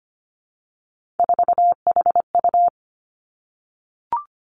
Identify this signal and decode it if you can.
Morse code, heard as a keyed tone, about 25 wpm, 700 Hz. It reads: 4HU
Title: Three Challenging Letter-Number Combinations 25wpm